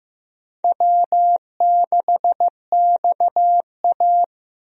Morse code, heard as a keyed tone, about 15 wpm, 700 Hz. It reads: W6XA
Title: USA Call Signs - 4 Characters - Repeated 3x - 15wpm